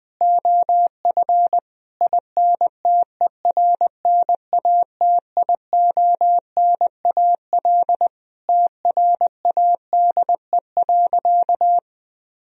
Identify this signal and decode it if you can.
Morse code, heard as a keyed tone, about 21 wpm, 700 Hz. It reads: OF INTERNATIONAL TRADE.
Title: Sentences from Top 1000 Words - Review Part 4 - 20wpm